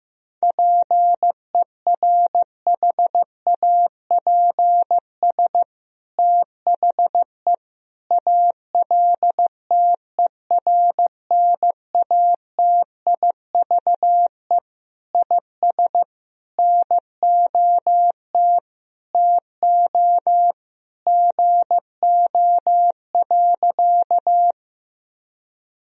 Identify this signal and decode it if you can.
Morse code, heard as a keyed tone, about 15 wpm, 700 Hz. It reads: PERHAPS THE ALTERNATIVE IS NOT TO GO.
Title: Sentences from Top 1000 Words - Review Part 4 - 15wpm